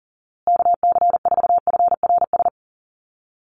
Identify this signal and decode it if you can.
Morse code, heard as a keyed tone, about 40 wpm, 700 Hz. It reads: KC4FRS
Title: USA Call Signs Repeated 3x 40wpm